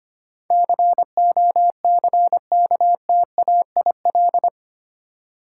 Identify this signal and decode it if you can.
Morse code, heard as a keyed tone, about 25 wpm, 700 Hz. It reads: COCKTAIL
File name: Eight Letter Words - 25wpm